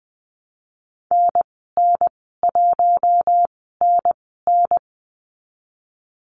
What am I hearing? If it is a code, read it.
Morse code, heard as a keyed tone, about 20 wpm, spaced as for 15 wpm, 700 Hz. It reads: NN1NN
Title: 048 Lesson 9g - Number 4 - Call Signs 20wpm